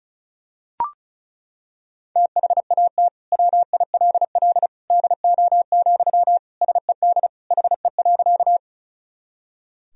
Morse code, heard as a keyed tone, about 34 wpm, 700 Hz. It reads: THAT WILL DO, SED HE.
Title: Sentences from Top 100 Words and Abbreviations 35wpm